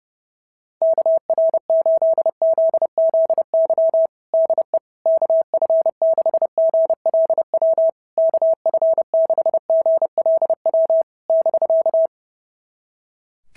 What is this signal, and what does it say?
Morse code, heard as a keyed tone, about 29 wpm, 650 Hz. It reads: KR8ZZY DE KF6GLW KF6GLW <BK>
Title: SOTA QSO Practice 30wpm